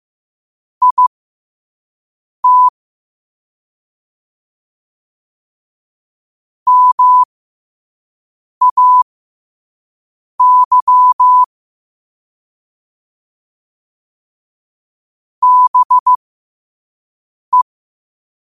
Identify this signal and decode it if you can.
Morse code, heard as a keyed tone, about 15 wpm, spaced as for 5 wpm, 1000 Hz. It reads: IT MAY BE
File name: qotdfiles / 5wpm